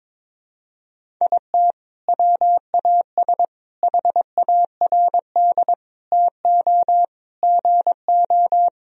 Morse code, heard as a keyed tone, about 22 wpm, 700 Hz. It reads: IT WAS HARD TO GO
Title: Sentences from Top 1000 Words - Review Part 1 - 22wpm